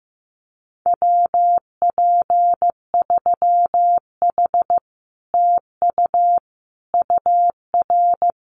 Morse code, heard as a keyed tone, about 15 wpm, 700 Hz. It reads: WP3H TU UR
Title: Common QSO Phrases Encore 15wpm